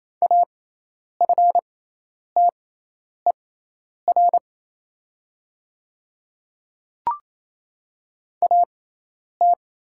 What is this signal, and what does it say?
Morse code, heard as a keyed tone, about 27 wpm, spaced as for 10 wpm, 700 Hz. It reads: AFTER AT
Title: 6x Character Spacing - Top 200 Words - 28wpm